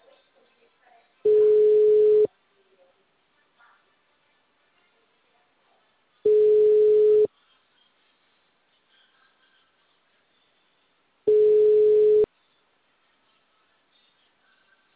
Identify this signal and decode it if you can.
Ringback tone (Europe, 425 Hz).